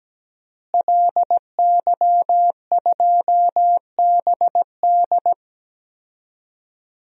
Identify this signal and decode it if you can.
Morse code, heard as a keyed tone, about 17 wpm, 700 Hz. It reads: LY2BD
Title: DXCC-100 Call Signs - 5 Characters - Repeated 3x - 17wpm